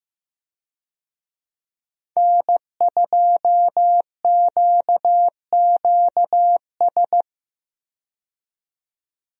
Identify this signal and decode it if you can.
Morse code, heard as a keyed tone, about 15 wpm, 700 Hz. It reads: N2QQS
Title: USA Call Signs - 5 Characters - 15wpm